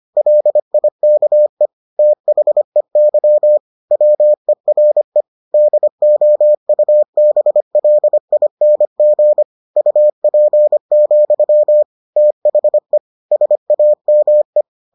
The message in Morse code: LIKE THEY WERE DOUBLING UP, THE SAME

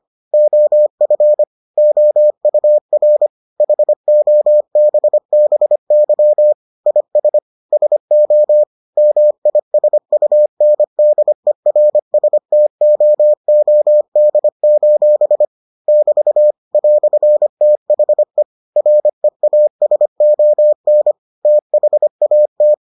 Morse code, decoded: OF OUR HOBBY IS SO MISUNDERSTOOD: <BT> "THE REASON THAT

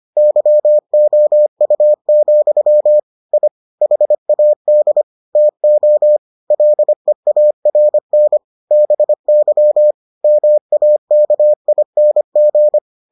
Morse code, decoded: YOU, I HAD TO LEARN BY MAKING

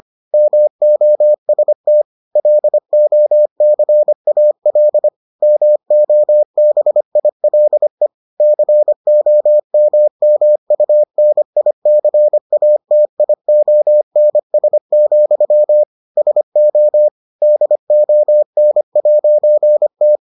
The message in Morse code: MOST LOCAL MOBILE COMMUNICATIONS, SO DON'T

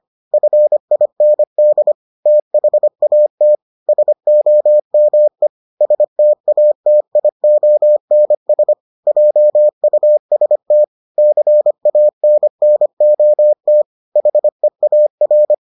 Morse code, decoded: FIND THAT SOME STATIONS JUST CANNOT HEAR

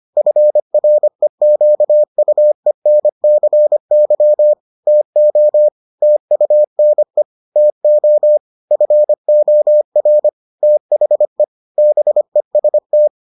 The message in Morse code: FREQUENCY TO TUNE TO FOR THE BEST

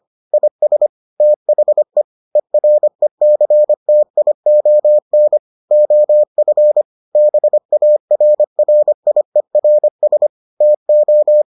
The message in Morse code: IS THE ERECTION OF BARRIERS TO

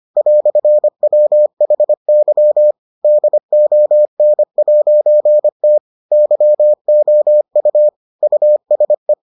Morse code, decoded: "WHY DON'T YOU USE

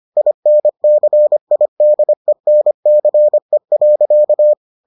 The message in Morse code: INCIDENCE.

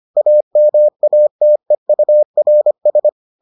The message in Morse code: AMATEURS

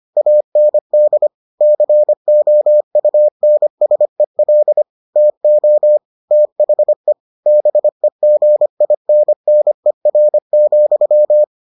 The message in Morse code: AND COUNSEL TO THE BEGINNER,

